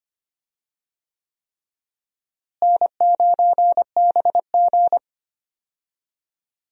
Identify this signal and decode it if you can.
Morse code, heard as a keyed tone, about 25 wpm, 700 Hz. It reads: N9BG
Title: USA Call Signs - 4 Characters - Repeated 3x - 25wpm